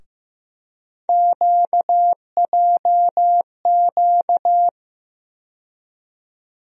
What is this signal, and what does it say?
Morse code, heard as a keyed tone, about 15 wpm, 700 Hz. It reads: QJQ
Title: USA Call Sign Postfixes - 3 Letters 15wpm